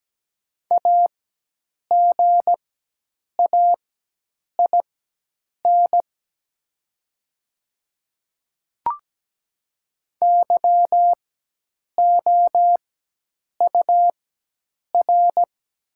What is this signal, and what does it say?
Morse code, heard as a keyed tone, about 17 wpm, spaced as for 8 wpm, 700 Hz. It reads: AGAIN YOUR
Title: 4x Character Spacing - Top 200 Words - 17wpm